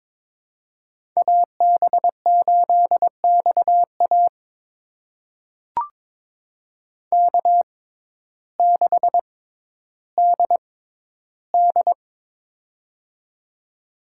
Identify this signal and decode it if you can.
Morse code, heard as a keyed tone, about 22 wpm, spaced as for 12 wpm, 700 Hz. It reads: AB8XA K6DD
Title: USA Call Signs - 6x Character Spacing - 22wpm